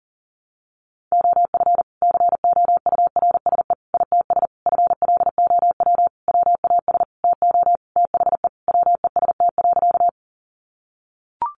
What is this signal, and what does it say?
Morse code, heard as a keyed tone, about 40 wpm, 700 Hz. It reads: OF COURSE ITS FLOW WAS TO THE WEST.
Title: Sentences from Top 1200 Words 40wpm